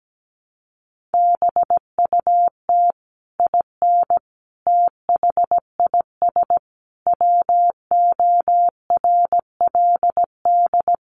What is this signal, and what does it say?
Morse code, heard as a keyed tone, about 17 wpm, 700 Hz. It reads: BUT IN THIS WORLD
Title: Sentences from Top 600 Words 17wpm